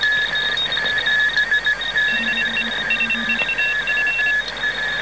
Morse code, heard as a keyed tone, about 25 wpm, 2700 Hz, with noise. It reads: DE SVF